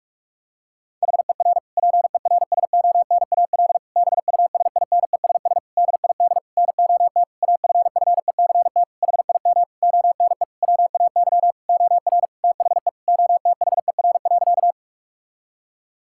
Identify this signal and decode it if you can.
Morse code, heard as a keyed tone, about 46 wpm, 700 Hz. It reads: HER PERSONAL BUSINESS DID NOT AFFECT HIM ONE WAY OR THE OTHER.
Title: Sentences from Top 1000 Words - Review Part 6 - 45wpm